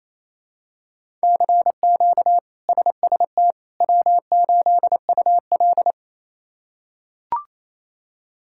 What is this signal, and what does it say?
Morse code, heard as a keyed tone, about 28 wpm, 700 Hz. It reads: CQ SST W8UL
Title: K1USN SST QSO Practice - 28wpm